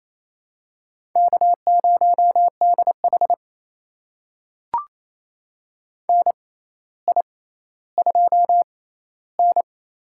Morse code, heard as a keyed tone, about 28 wpm, spaced as for 14 wpm, 700 Hz. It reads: K0DH NI2N
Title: USA Call Signs - 4 Characters - 6x Character Spacing - 28wpm